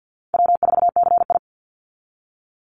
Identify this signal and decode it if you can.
Morse code, heard as a keyed tone, about 50 wpm, 700 Hz. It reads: W4CI
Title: Easy Call Signs Repeated 2x - 50wpm